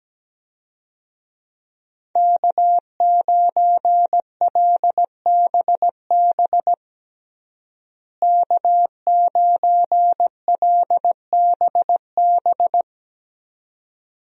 Morse code, heard as a keyed tone, about 17 wpm, 700 Hz. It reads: K9LBB K9LBB
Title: USA Call Signs - 5 Characters - Repeated 2x - 17wpm